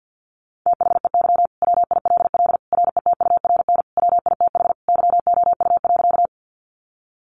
Morse code, heard as a keyed tone, about 50 wpm, 700 Hz. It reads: THEY WILL RETURN WITH YOU.
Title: Sentences from Top 800 Words 50wpm